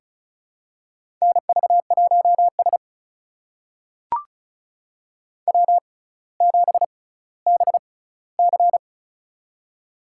Morse code, heard as a keyed tone, about 34 wpm, spaced as for 18 wpm, 700 Hz. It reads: NV1S W7BC
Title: USA Call Signs - 4 Characters - 6x Character Spacing - 35wpm